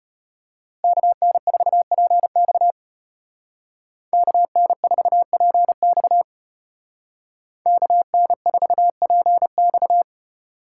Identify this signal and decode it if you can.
Morse code, heard as a keyed tone, about 33 wpm, 700 Hz. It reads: KN4PX KN4PX KN4PX